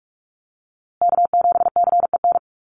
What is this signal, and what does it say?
Morse code, heard as a keyed tone, about 44 wpm, 700 Hz. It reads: K7CEN
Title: USA Call Signs Repeated 3x 45wpm